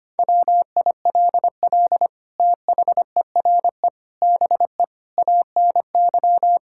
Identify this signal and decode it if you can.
Morse code, heard as a keyed tone, about 25 wpm, 700 Hz. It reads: WILL THERE BE ANY
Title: Sentences from Top 1600 Words 25wpm